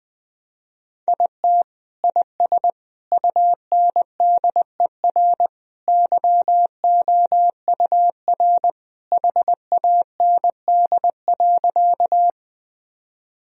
Morse code, heard as a keyed tone, about 21 wpm, 700 Hz. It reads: IT IS UNDER YOUR HAND.
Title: Sentences from Top 1000 Words - Review Part 1 - 20wpm